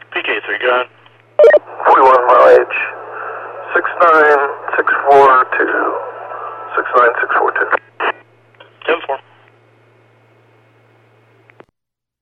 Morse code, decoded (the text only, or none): none